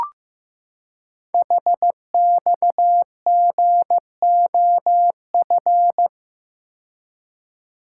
Morse code, heard as a keyed tone, about 15 wpm, 700 Hz. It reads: HXGOF
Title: Code Groups - 5 Letters - 15wpm